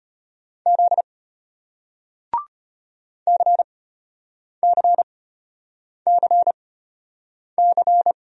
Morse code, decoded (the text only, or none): Z CCCC